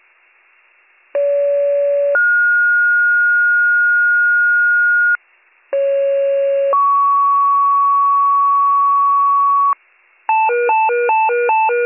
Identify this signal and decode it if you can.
Two-tone page, heard as a one-second tone, then a three-second tone, about 570 then 1430 Hz, about 550 then 1120 Hz, with noise. It is heard from 1.1 to 9.7 s.